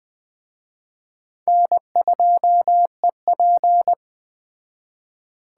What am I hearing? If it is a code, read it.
Morse code, heard as a keyed tone, about 20 wpm, 700 Hz. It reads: N2EP
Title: Speed Racing - USA 1x2 Call Signs 17wpm